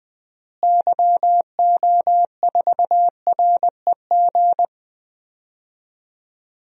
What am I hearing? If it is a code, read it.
Morse code, heard as a keyed tone, about 21 wpm, 700 Hz. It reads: YO4REG
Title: DXCC-100 Call Signs - 6 Characters - Repeated 3x - 20wpm